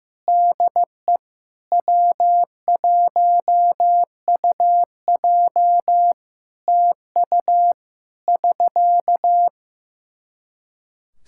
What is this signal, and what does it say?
Morse code, heard as a keyed tone, about 15 wpm, 700 Hz. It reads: DE W1UJ TU <SK>
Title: K1USN SST QSO Practice 15wpm